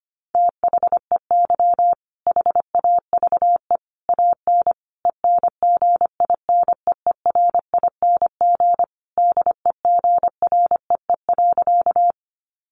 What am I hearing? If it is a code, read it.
Morse code, heard as a keyed tone, about 25 wpm, 700 Hz. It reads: THEY HAVE AN ENGINEERING DEGREE.
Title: Sentences from Top 1500 Words 25wpm